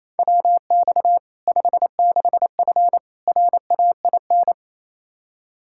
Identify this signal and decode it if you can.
Morse code, heard as a keyed tone, about 28 wpm, 700 Hz. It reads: WX 56F RAIN
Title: Practice QSO - Exchange 2 - Info - 28wpm